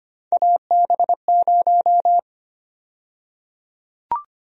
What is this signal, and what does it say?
Morse code, heard as a keyed tone, about 25 wpm, 700 Hz. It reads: AB0
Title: USA Call Sign Prefixes - Repeated 2x - 25wpm